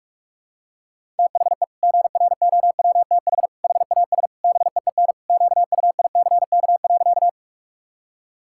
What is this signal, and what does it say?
Morse code, heard as a keyed tone, about 46 wpm, 690 Hz. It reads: THE GROWTH HAS BEEN QUICK.
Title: Sentences from Top 1000 Words - Review Part 5 - 45wpm